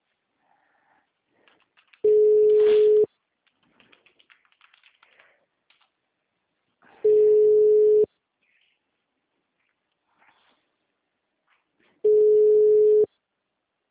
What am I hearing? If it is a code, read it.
Ringback tone (Europe, 425 Hz).